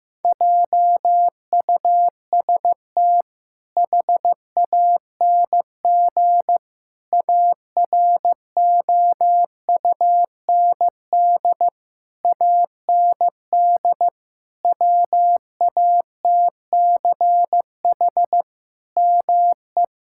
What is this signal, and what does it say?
Morse code, heard as a keyed tone, about 15 wpm, 700 Hz. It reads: JUST HANG AROUND AND WATCH ME
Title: Sentences from Top 1600 Words 15wpm